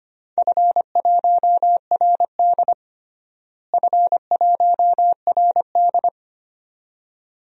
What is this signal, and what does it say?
Morse code, heard as a keyed tone, about 25 wpm, 700 Hz. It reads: F1RD F1RD
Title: DXCC-100 Call Signs - 4 Characters - Repeated 2x - 25wpm